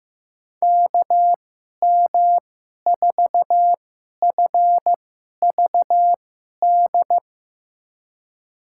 Morse code, heard as a keyed tone, about 15 wpm, spaced as for 11 wpm, 700 Hz. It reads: KM4FVD
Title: USA Call Signs - 6 Characters - 2x Character Spacing - 15wpm